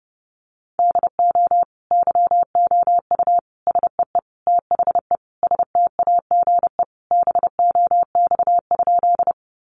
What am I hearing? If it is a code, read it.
Morse code, heard as a keyed tone, about 30 wpm, 700 Hz. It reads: DO YOU SEE THE STAGE BOX?
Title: Sentences from Top 1200 Words 30wpm